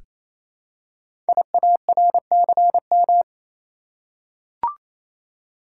Morse code, heard as a keyed tone, about 27 wpm, 700 Hz. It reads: IARCM
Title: Code Groups - 5 Letters - 28wpm